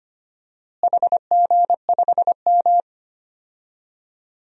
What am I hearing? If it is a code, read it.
Morse code, heard as a keyed tone, about 25 wpm, 700 Hz. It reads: HG5M
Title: DXCC-100 Call Signs - 4 Characters - Repeated 2x - 25wpm